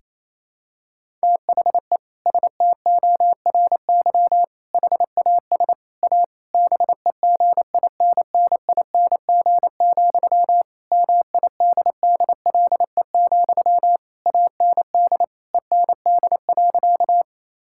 Morse code, heard as a keyed tone, about 28 wpm, 700 Hz. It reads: THE STORY HAS A BEGINNING, MIDDLE, AND END.
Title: Sentences from Top 1000 Words - Review Part 6 - 28wpm